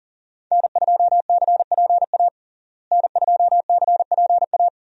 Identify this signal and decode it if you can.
Morse code, heard as a keyed tone, about 40 wpm, 700 Hz. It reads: N2CPA N2CPA
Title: USA Call Signs - 5 Characters - Repeated 3x - 40wpm